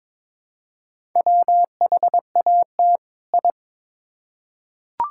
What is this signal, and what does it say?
Morse code, heard as a keyed tone, about 22 wpm, 700 Hz. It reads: WHAT I
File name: Build A Phrase - 2-Word Phrases - 22wpm